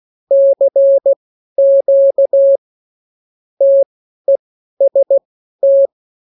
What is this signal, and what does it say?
Morse code, heard as a keyed tone, about 16 wpm, spaced as for 12 wpm, 550 Hz. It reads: CQ TEST
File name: CWconcurso QSO libroCW4 16-8ppm